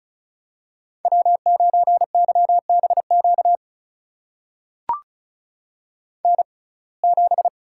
Morse code, heard as a keyed tone, about 36 wpm, spaced as for 25 wpm, 700 Hz. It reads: W9YBQ N7
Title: USA Call Signs - 5 Characters - 6x Character Spacing - 35wpm